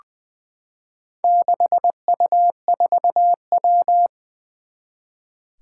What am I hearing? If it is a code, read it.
Morse code, heard as a keyed tone, about 20 wpm, 700 Hz. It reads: 6U4W
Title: Four Challenging Letter-Number Combinations 20wpm